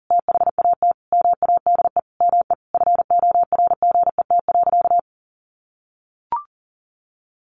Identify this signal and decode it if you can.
Morse code, heard as a keyed tone, about 40 wpm, 700 Hz. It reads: THAT MADE ME FORGET.
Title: Sentences from Top 1500 Words 40wpm